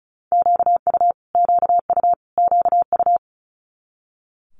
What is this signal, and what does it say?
Morse code, heard as a keyed tone, about 34 wpm, 700 Hz. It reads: QU QU QU
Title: Multi-letter Phonemes Repeated 3x 35wpm